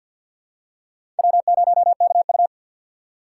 Morse code, heard as a keyed tone, about 52 wpm, 700 Hz. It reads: W0KU
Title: USA Call Signs - 6x Character Spacing - 50wpm